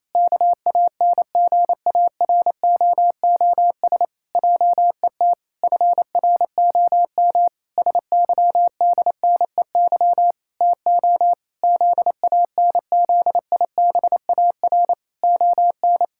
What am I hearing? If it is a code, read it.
Morse code, heard as a keyed tone, about 27 wpm, 700 Hz. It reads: KANGAROOS JET FROM SYDNEY TO ZANZIBAR ON
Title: Pangram Challenge v2 - 28wpm